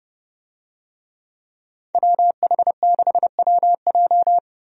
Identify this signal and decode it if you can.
Morse code, heard as a keyed tone, about 32 wpm, 700 Hz. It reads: WH6WJ
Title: USA Call Signs - 5 Characters - 30wpm